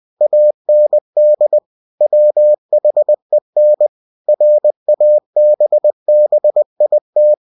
Morse code, heard as a keyed tone, about 20 wpm, 600 Hz. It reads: AND WHEN RABBIT
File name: cw-books / winnie-the-pooh / winnie-20wpm-page-0062